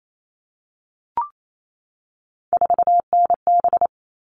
Morse code, heard as a keyed tone, about 28 wpm, 700 Hz. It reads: 4NB
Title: All Cut Numbers - 3-Digit 28wpm